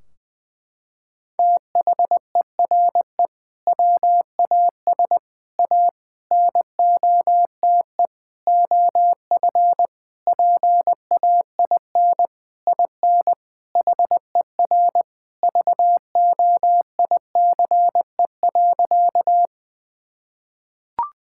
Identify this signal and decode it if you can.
Morse code, heard as a keyed tone, about 20 wpm, 700 Hz. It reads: THERE WAS A NOTE OF PAIN IN HER VOICE.
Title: Sentences from Top 1600 Words 20wpm